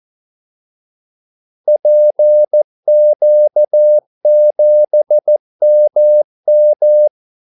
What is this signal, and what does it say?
Morse code, heard as a keyed tone, about 14 wpm, 600 Hz. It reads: PQ7MM
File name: Callsigns-14wpm